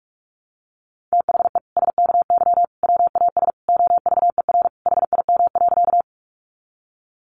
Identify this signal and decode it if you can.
Morse code, heard as a keyed tone, about 44 wpm, 700 Hz. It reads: THE SKY WAS OVER HIM.
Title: Sentences from Top 1200 Words 45wpm